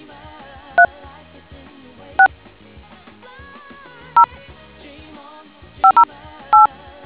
DTMF, heard as telephone touch tones, tones of about 100 ms. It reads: 3 6 0 508